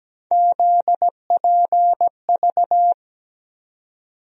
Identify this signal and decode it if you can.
Morse code, heard as a keyed tone, about 17 wpm, 700 Hz. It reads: ZPV
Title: USA Call Sign Postfixes - 3 Letters 17wpm